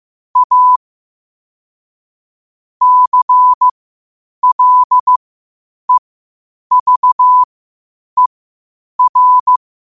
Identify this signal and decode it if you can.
Morse code, heard as a keyed tone, about 15 wpm, spaced as for 8 wpm, 1000 Hz. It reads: A CLEVER
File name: qotdfiles / 8wpm